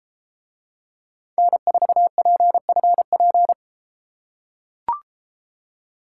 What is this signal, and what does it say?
Morse code, heard as a keyed tone, about 33 wpm, 700 Hz. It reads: N4PFP